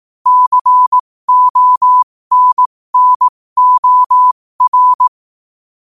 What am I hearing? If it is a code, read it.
Morse code, heard as a keyed tone, about 18 wpm, spaced as for 15 wpm, 1000 Hz. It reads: CONNOR